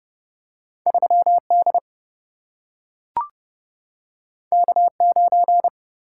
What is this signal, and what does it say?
Morse code, heard as a keyed tone, about 29 wpm, 700 Hz. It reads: 3 D K 9
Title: Single-Letter Single-Number 30wpm